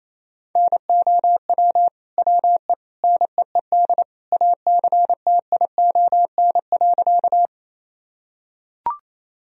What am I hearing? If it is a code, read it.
Morse code, heard as a keyed tone, about 28 wpm, 700 Hz. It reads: NOW WE NEED ACTION.
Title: Sentences from Top 1000 Words - Review Part 4 - 28wpm